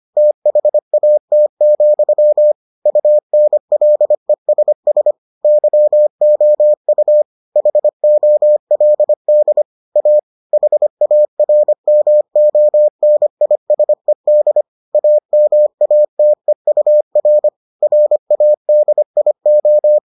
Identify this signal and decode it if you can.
Morse code, heard as a keyed tone, about 25 wpm, 600 Hz. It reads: THAT, UNLESS YOU HOLD A HARMONISED AMATEUR RADIO